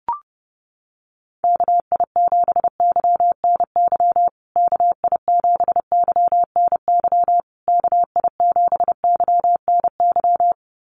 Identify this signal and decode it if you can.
Morse code, heard as a keyed tone, about 30 wpm, 700 Hz. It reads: KI7YNY KI7YNY KI7YNY
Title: USA Call Signs Repeated 3x 30wpm